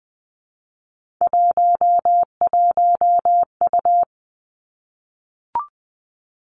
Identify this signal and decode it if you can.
Morse code, heard as a keyed tone, about 20 wpm, 700 Hz. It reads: 11U
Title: Three Challenging Letter-Number Combinations 20wpm